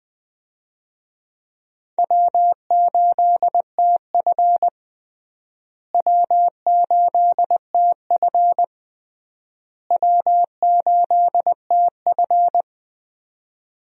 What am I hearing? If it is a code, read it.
Morse code, heard as a keyed tone, about 20 wpm, 700 Hz. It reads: W8TF W8TF W8TF
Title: USA Call Signs - 4 Characters - Repeated 3x - 20wpm